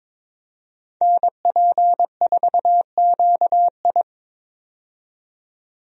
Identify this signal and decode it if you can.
Morse code, heard as a keyed tone, about 22 wpm, 700 Hz. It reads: NP4QI
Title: USA Call Signs - 5 Characters - 22wpm